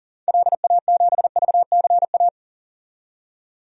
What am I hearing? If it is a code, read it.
Morse code, heard as a keyed tone, about 39 wpm, 700 Hz. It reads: LA7VCA
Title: DXCC-100 Call Signs - 6 Characters - 40wpm